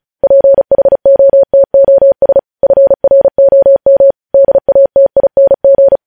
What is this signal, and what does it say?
Morse code, heard as a keyed tone, about 36 wpm, 550 Hz. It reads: PHOTOS FROM DATING
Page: News headlines in Morse code at 35 wpm